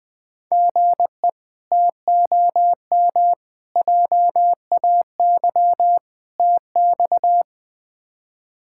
Morse code, heard as a keyed tone, about 20 wpm, 700 Hz. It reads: GE TOM JAY TX
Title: K1USN SST QSO Practice - 20wpm